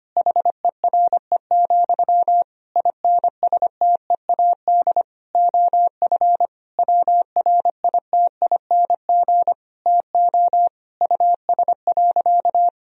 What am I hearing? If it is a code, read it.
Morse code, heard as a keyed tone, about 25 wpm, 700 Hz. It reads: HERE, INSTEAD OF WRITING TO US.
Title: Sentences from Top 1000 Words - Review Part 4 - 25wpm